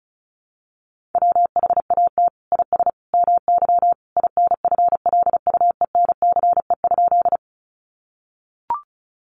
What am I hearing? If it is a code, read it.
Morse code, heard as a keyed tone, about 34 wpm, 700 Hz. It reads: WHAT IS MY INFLUENCE?
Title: Sentences from Top 1100 Words 35wpm